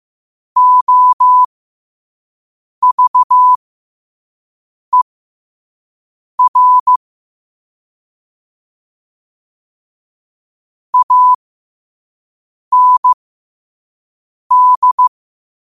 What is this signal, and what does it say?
Morse code, heard as a keyed tone, about 15 wpm, spaced as for 5 wpm, 1000 Hz. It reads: OVER AND
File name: qotdfiles / 5wpm